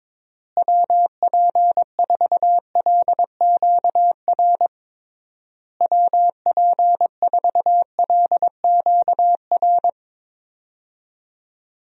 Morse code, heard as a keyed tone, about 22 wpm, 700 Hz. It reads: WP4LQR WP4LQR
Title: USA Call Signs - 6 Characters - Repeated 3x - 22wpm